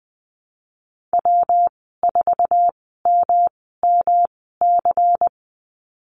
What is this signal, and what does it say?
Morse code, heard as a keyed tone, about 20 wpm, spaced as for 15 wpm, 700 Hz. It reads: W4MMC
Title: 147 Lesson 24g - Letter y - Call Signs 20wpm